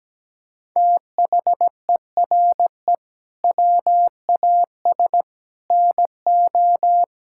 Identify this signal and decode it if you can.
Morse code, heard as a keyed tone, about 17 wpm, 700 Hz. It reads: THERE WAS NO
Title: Sentences from Top 1600 Words 17wpm